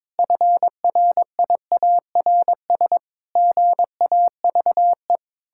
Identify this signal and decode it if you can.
Morse code, heard as a keyed tone, about 21 wpm, 700 Hz. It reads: FRIARS GAVE
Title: Pangram Challenge v2 - 22wpm